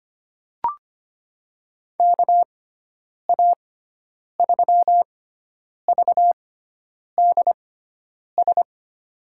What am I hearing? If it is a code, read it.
Morse code, heard as a keyed tone, about 25 wpm, spaced as for 9 wpm, 700 Hz. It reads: KA3VDS
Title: USA Call Signs - 6 Characters - 6x Character Spacing - 25wpm